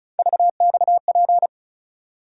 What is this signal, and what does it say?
Morse code, heard as a keyed tone, about 36 wpm, 700 Hz. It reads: VXP
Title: USA Call Sign Postfixes - 3 Letters 35wpm